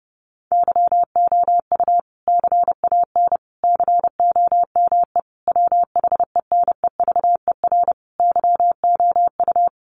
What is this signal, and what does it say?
Morse code, heard as a keyed tone, about 28 wpm, 700 Hz. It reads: YOU CAN COME WHENEVER YOU
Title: Sentences from Top 1200 Words 30wpm